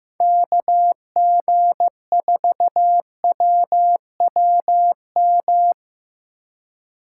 Morse code, heard as a keyed tone, about 18 wpm, 700 Hz. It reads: KG4WWM
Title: Speed Racing - USA 2x3 Call Signs 15wpm